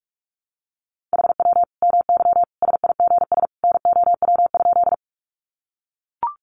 Morse code, heard as a keyed tone, about 44 wpm, 700 Hz. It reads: HW MY SIGS NOW?
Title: Common QSO Phrases 45wpm